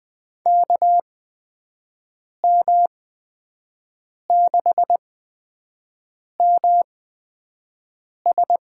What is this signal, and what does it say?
Morse code, heard as a keyed tone, about 20 wpm, spaced as for 5 wpm, 700 Hz. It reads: KM6MS
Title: USA Call Signs - 6 Characters - 8x Character Spacing - 20wpm